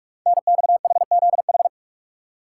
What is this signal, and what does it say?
Morse code, heard as a keyed tone, about 44 wpm, 700 Hz. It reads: NXHZH
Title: Code Groups - 5 Letters - 45wpm